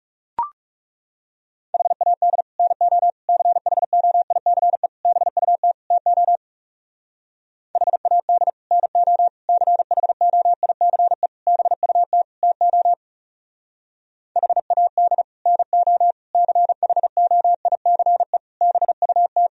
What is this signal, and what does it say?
Morse code, heard as a keyed tone, about 35 wpm, 700 Hz. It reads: HAD NO CHOICE BUT TO HAD NO CHOICE BUT TO HAD NO CHOICE BUT
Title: Speed-Racing - Sets of 5 Words 30wpm